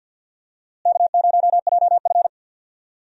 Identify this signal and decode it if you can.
Morse code, heard as a keyed tone, about 52 wpm, 700 Hz. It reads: K0JF
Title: USA Call Signs - 4 Characters - Repeated 2x - 50wpm